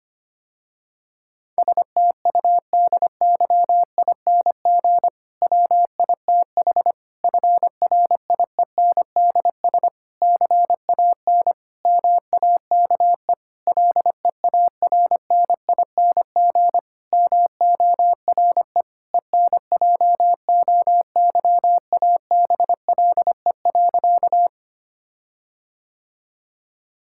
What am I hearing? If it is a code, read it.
Morse code, heard as a keyed tone, about 25 wpm, 700 Hz. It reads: STUDYING WITH FRIENDS CAN MAKE LEARNING MORE ENJOYABLE.
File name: practice / 25wpm / Difficult Sentences with Hints - 25wpm